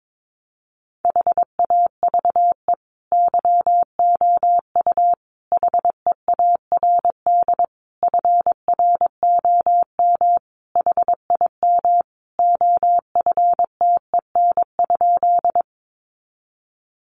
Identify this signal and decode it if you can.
Morse code, heard as a keyed tone, about 22 wpm, 700 Hz. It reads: HAVE YOU HEARD FROM HIM OFTEN?
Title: Sentences from Top 400 Words 22wpm